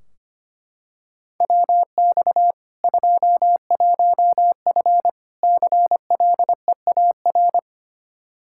Morse code, heard as a keyed tone, about 25 wpm, 700 Hz. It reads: WX 21F CLEAR
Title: Practice QSO - Exchange 2 - Info - 25wpm